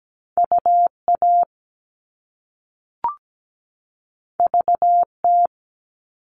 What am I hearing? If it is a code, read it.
Morse code, heard as a keyed tone, about 17 wpm, 700 Hz. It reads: UA VT
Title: All Cut Numbers - 2-Digit 17wpm